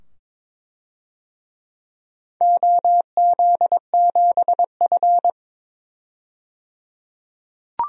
Morse code, heard as a keyed tone, about 22 wpm, 700 Hz. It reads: OZ7F
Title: DXCC-100 Call Signs - 4 Characters - 22wpm